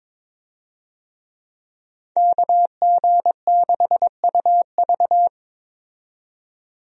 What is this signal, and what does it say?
Morse code, heard as a keyed tone, about 22 wpm, 700 Hz. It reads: KG6UV
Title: USA Call Signs - 5 Characters - 22wpm